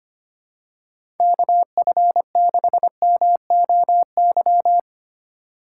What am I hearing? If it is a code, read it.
Morse code, heard as a keyed tone, about 25 wpm, 700 Hz. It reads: KF6MOY
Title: USA Call Signs - 6 Characters - 3x Character Spacing - 25wpm